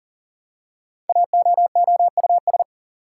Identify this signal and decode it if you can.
Morse code, heard as a keyed tone, about 38 wpm, 700 Hz. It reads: AOOUS
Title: Code Groups - 5 Letters - 40wpm